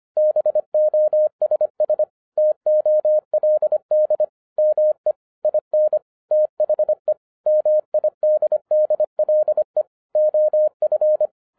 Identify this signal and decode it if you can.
Morse code, heard as a keyed tone, about 25 wpm, 610 Hz. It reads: BOSS TOLD ME IN THE MIDDLE OF
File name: 2011-10-mp3 / n04-JUST MOVED 25wpm